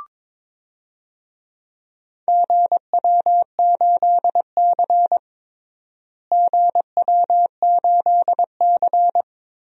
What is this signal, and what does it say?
Morse code, heard as a keyed tone, about 21 wpm, 700 Hz. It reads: GW8C GW8C
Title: DXCC-100 Call Signs - 4 Characters - Repeated 2x - 22wpm